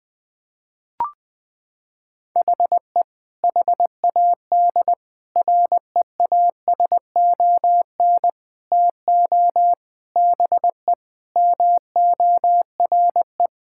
Morse code, heard as a keyed tone, about 21 wpm, 700 Hz. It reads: HE HAD REASON TO BE MORE
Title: Sentences from Top 1000 Words - Review Part 6 - 20wpm